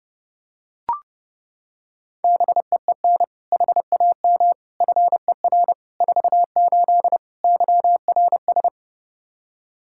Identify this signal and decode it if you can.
Morse code, heard as a keyed tone, about 30 wpm, 700 Hz. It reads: BEEN HAM FER 48 YRS
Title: Practice QSO - Exchange 3 - More Info - 30wpm